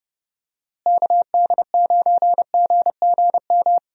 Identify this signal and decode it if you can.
Morse code, heard as a keyed tone, about 30 wpm, 700 Hz. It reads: KD9GGM
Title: USA Call Signs - 6 Characters - Repeated 3x - 30wpm